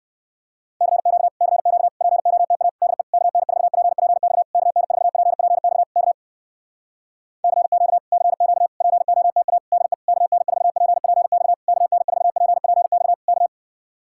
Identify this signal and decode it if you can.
Morse code, heard as a keyed tone, about 64 wpm, 700 Hz. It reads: CQ CQ POTA DE KN4PPX KN4PPX K CQ CQ POTA DE KN4PPX KN4PPX K